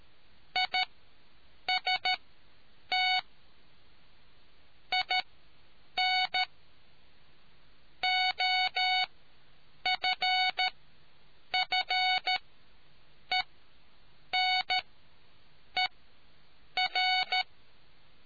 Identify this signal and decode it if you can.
Morse code, heard as a keyed tone, about 13 wpm, spaced as for 7 wpm, 740 Hz. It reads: IST IN OFFENER